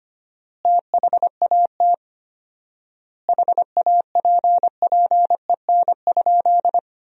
Morse code, heard as a keyed tone, about 25 wpm, 700 Hz. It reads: THAT HAPPEN?
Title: 4x Word-Spacing - Sentences from Top 400 Words - 25wpm